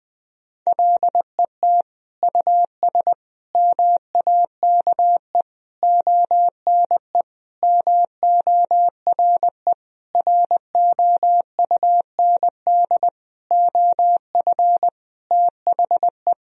Morse code, heard as a keyed tone, about 21 wpm, 700 Hz. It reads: LET US MAKE ONE MORE ROUND OF THE
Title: Sentences from Top 1000 Words - Review Part 4 - 20wpm